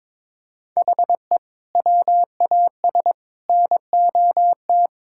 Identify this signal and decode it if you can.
Morse code, heard as a keyed tone, about 22 wpm, 700 Hz. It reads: HE WAS NOT
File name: practice / 22wpm / Sentences with 3-Word Phrases - 22wpm